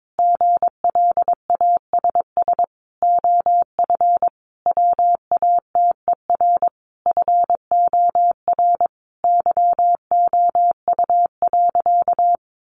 Morse code, heard as a keyed tone, about 22 wpm, 700 Hz. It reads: GLASS OF WATER FOR YOU.
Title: Sentences from Top 1300 Words 22wpm